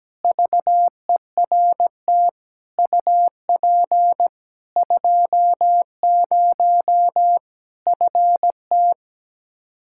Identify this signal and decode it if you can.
Morse code, heard as a keyed tone, about 17 wpm, 700 Hz. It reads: VERT UP 20 FT
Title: Practice QSO - Exchange 2 - Info - 17wpm